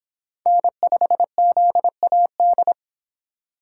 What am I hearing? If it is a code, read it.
Morse code, heard as a keyed tone, about 25 wpm, 700 Hz. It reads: N5ZAD